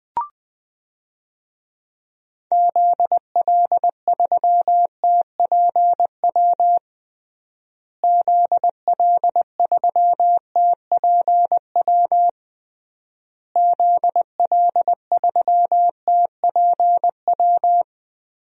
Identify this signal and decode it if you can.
Morse code, heard as a keyed tone, about 21 wpm, 700 Hz. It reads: ZL3TPW ZL3TPW ZL3TPW
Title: DXCC-100 Call Signs - 6 Characters - Repeated 3x - 20wpm